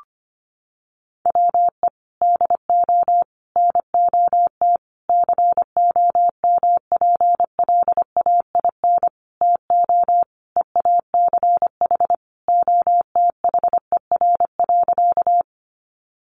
Morse code, heard as a keyed tone, about 25 wpm, 700 Hz. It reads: WE DO NOT COMPLAIN TO EACH OTHER.
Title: Sentences from Top 1300 Words 25wpm